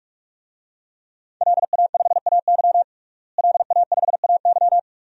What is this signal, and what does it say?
Morse code, heard as a keyed tone, about 46 wpm, 700 Hz. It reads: LA5AY LA5AY
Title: DXCC-100 Call Signs - 5 Characters - Repeated 2x - 45wpm